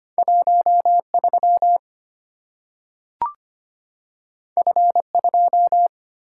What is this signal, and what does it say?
Morse code, heard as a keyed tone, about 25 wpm, 700 Hz. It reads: 1 3 F 2
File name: Two Letter-Number Combinations - 25wpm